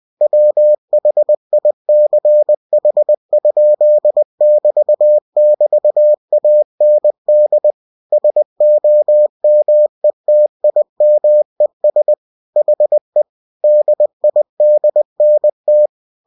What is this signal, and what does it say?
Morse code, heard as a keyed tone, about 20 wpm, 600 Hz. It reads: WHICH?<BT><BT>AND SOMETIMES HE DIDNT
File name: cw-books / winnie-the-pooh / winnie-20wpm-page-0026